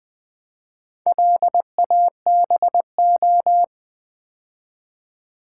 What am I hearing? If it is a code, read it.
Morse code, heard as a keyed tone, about 20 wpm, 700 Hz. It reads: LABO
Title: Build A Word - 4 Letters At A Time - 10-Letter Words - 20wpm